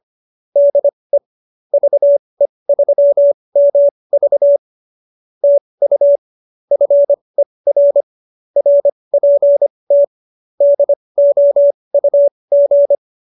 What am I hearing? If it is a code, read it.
Morse code, heard as a keyed tone, about 25 wpm, spaced as for 20 wpm, 570 Hz. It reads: DE VE3MV TU FER RPT DOUG
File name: qso220b 20